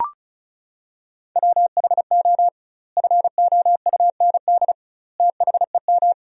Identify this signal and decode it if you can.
Morse code, heard as a keyed tone, about 36 wpm, 700 Hz. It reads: WHO FOUND THEM
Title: 2x Word-Spacing - Sentences from Top 300 Words - 35wpm